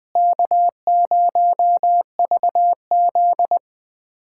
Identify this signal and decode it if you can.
Morse code, heard as a keyed tone, about 20 wpm, 700 Hz. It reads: K0VZ
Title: USA Callsign DE Callsign - 4 Characters - 20wpm